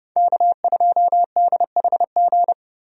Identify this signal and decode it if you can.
Morse code, heard as a keyed tone, about 29 wpm, 700 Hz. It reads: K2DHG
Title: USA Call Signs - 5 Characters - 30wpm